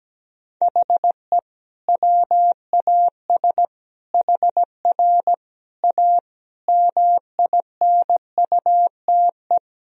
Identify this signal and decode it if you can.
Morse code, heard as a keyed tone, about 17 wpm, 700 Hz. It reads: HE WAS HR A MINUTE
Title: Sentences from Top 500 Words and Abbreviations 17wpm